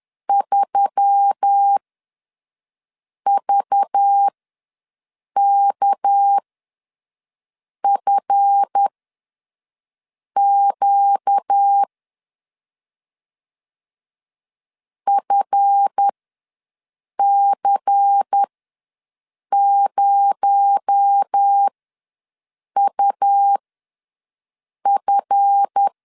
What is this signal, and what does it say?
Morse code, heard as a keyed tone, about 10 wpm, spaced as for 5 wpm, 800 Hz. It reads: 3VKFQ FC0UF
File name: LEZIONI / 7 GF9Z / 7 GF ws 10wpm 3918